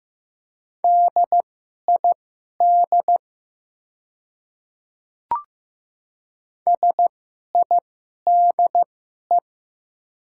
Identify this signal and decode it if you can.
Morse code, heard as a keyed tone, about 15 wpm, spaced as for 11 wpm, 700 Hz. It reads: DID SIDE
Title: 2x Character Spacing - Top 100 Words - 15wpm